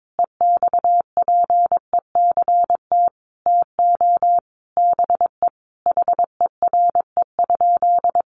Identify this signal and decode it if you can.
Morse code, heard as a keyed tone, about 22 wpm, 700 Hz. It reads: EXPECT TO BE HERE?
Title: Sentences from Top 1100 Words 22wpm